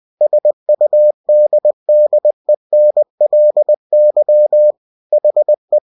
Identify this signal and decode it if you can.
Morse code, heard as a keyed tone, about 20 wpm, 600 Hz. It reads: SUDDENLY HE
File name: cw-books / winnie-the-pooh / winnie-20wpm-page-0039